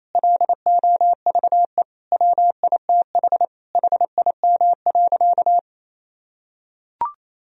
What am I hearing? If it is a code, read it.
Morse code, heard as a keyed tone, about 28 wpm, 700 Hz. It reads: LOVE WITH HIM.
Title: Sentences from Top 1000 Words - Review Part 4 - 28wpm